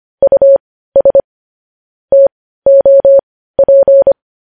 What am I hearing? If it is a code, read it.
Morse code, heard as a keyed tone, about 25 wpm, spaced as for 15 wpm, 550 Hz. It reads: US TOP